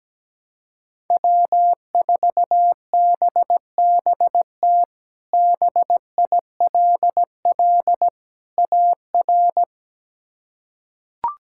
Morse code, heard as a keyed tone, about 17 wpm, 700 Hz. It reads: W4BBT BILL AR
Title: K1USN SST QSO Practice - 17wpm